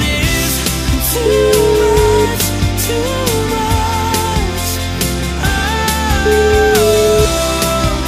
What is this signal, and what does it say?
Ringback tone (Europe, 425 Hz), with noise.